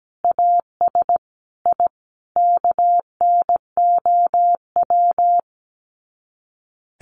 Morse code, heard as a keyed tone, about 17 wpm, 700 Hz. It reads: AS I KNOW
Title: Sets of 5 Words 17wpm